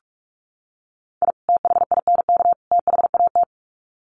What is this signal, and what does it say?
Morse code, heard as a keyed tone, about 44 wpm, 700 Hz. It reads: I THINK THAT
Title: Sets of 3 Words 45wpm